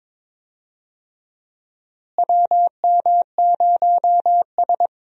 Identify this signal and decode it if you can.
Morse code, heard as a keyed tone, about 21 wpm, 700 Hz. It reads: WM0S
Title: USA Call Signs - 4 Characters - Repeated 3x - 22wpm